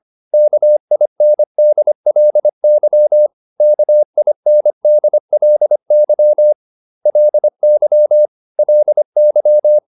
Morse code, heard as a keyed tone, about 25 wpm, 600 Hz. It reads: KINDLY KINDLY LY LY